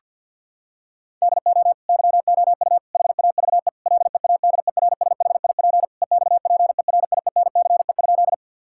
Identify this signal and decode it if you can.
Morse code, heard as a keyed tone, about 52 wpm, 700 Hz. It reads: DO YOU HAVE LEADERSHIP EXPERIENCE?
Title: Sentences from Top 1600 Words 50wpm